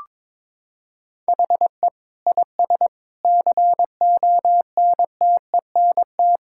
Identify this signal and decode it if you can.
Morse code, heard as a keyed tone, about 22 wpm, 700 Hz. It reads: HE IS CONTENT
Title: Sentences from Top 1000 Words - Review Part 6 - 22wpm